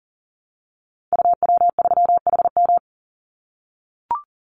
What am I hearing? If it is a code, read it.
Morse code, heard as a keyed tone, about 40 wpm, 700 Hz. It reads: UW3HM
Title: Call Signs - 40wpm